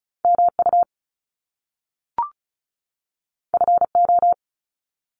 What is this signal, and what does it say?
Morse code, heard as a keyed tone, about 36 wpm, 700 Hz. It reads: MU FO
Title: Top Two Letter Combinations in Words 35wpm